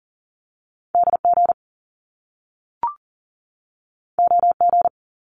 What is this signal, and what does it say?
Morse code, heard as a keyed tone, about 38 wpm, 700 Hz. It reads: DG OG
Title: Top Two Letter Combinations in Words 40wpm